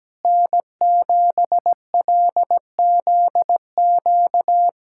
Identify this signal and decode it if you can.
Morse code, heard as a keyed tone, about 17 wpm, 700 Hz. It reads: N7LZQ
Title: USA Callsign DE Callsign - 5 Characters - 17wpm